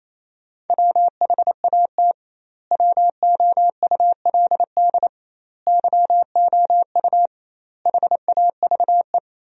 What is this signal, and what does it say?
Morse code, heard as a keyed tone, about 28 wpm, 700 Hz. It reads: WHAT WOULD YOU HAVE
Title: 2x Word-Spacing - Sentences from Top 300 Words - 28wpm